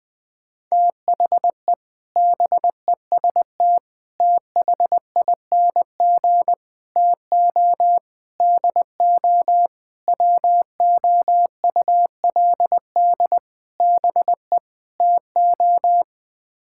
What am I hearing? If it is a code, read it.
Morse code, heard as a keyed tone, about 20 wpm, 700 Hz. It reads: THE BEST THING TO DO WOULD BE TO
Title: Sets of 8 Words - 20wpm